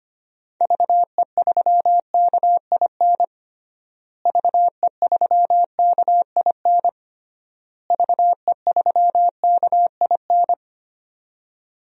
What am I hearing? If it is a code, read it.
Morse code, heard as a keyed tone, about 25 wpm, 700 Hz. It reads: VE3KIN VE3KIN VE3KIN
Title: DXCC-100 Call Signs - 6 Characters - Repeated 3x - 25wpm